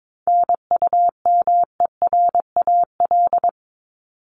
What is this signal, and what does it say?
Morse code, heard as a keyed tone, about 22 wpm, 700 Hz. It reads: NUMERAL
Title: Top 500 Words 22wpm